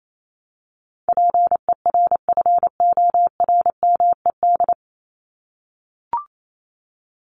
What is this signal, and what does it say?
Morse code, heard as a keyed tone, about 28 wpm, 700 Hz. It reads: PERFORMED
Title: ED Suffix - Top 100 Words with Base Words 28wpm